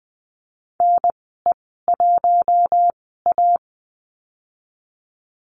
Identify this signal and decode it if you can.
Morse code, heard as a keyed tone, about 20 wpm, spaced as for 14 wpm, 700 Hz. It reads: NE1A
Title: 042 Lesson 8g - Number 1 - Call Signs 20wpm